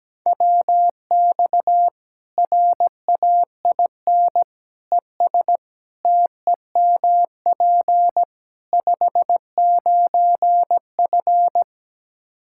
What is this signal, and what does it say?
Morse code, heard as a keyed tone, about 17 wpm, 700 Hz. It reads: WX RAIN ES TEMP 59F
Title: Practice QSO - Exchange 2 - Info - 17wpm